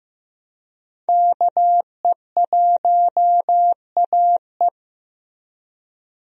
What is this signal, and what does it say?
Morse code, heard as a keyed tone, about 16 wpm, 700 Hz. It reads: KE1AE
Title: USA Call Signs - 5 Characters - Repeated 2x - 15wpm